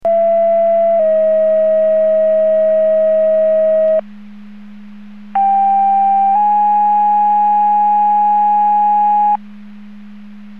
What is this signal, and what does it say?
Two-tone page, heard as a one-second tone, then a three-second tone, about 670 then 640 Hz, about 790 then 830 Hz. It is heard from 0.0 to 9.4 s.